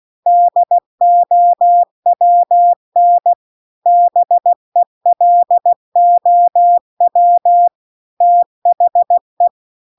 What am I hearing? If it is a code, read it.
Morse code, heard as a keyed tone, about 16 wpm, 700 Hz. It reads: DOWN BELOW THE